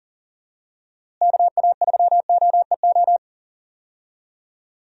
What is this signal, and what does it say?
Morse code, heard as a keyed tone, about 40 wpm, 700 Hz. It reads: KA3OEO
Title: USA Call Signs - 6 Characters - Repeated 2x - 40wpm